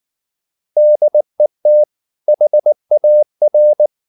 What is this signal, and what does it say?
Morse code, heard as a keyed tone, about 19 wpm, 600 Hz. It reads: DET HAR
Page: Ferie 19 wpm